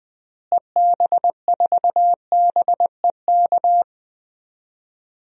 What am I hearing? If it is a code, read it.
Morse code, heard as a keyed tone, about 20 wpm, 700 Hz. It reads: EB4BEK
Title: DXCC-100 Call Signs - 6 Characters - Repeated 3x - 20wpm